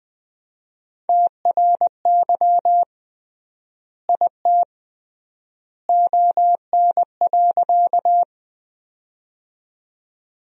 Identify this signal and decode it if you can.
Morse code, heard as a keyed tone, about 20 wpm, 700 Hz. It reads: TRY IT ON.
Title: 3x Word-Spacing - Sentences from Top 200 Words - 20wpm